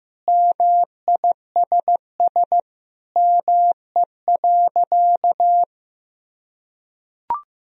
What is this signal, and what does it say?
Morse code, heard as a keyed tone, about 15 wpm, 700 Hz. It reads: MISS ME.
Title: Sentences from Top 1000 Words - Review Part 3 - 15wpm